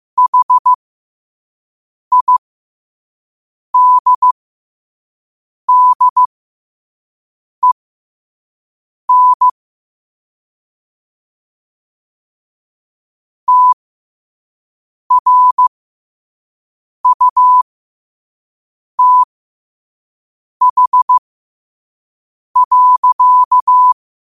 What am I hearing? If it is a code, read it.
Morse code, heard as a keyed tone, about 15 wpm, spaced as for 5 wpm, 1000 Hz. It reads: HIDDEN TRUTH.
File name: qotdfiles / 5wpm